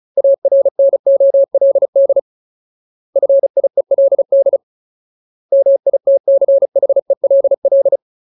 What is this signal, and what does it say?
Morse code, heard as a keyed tone, about 34 wpm, 550 Hz. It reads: ARNOLD FIELD MITCHELL
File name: Ultimate Word List / 35WPM Ultimate